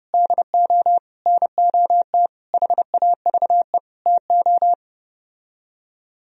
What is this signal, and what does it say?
Morse code, heard as a keyed tone, about 33 wpm, 700 Hz. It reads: DO NOT HAVE TO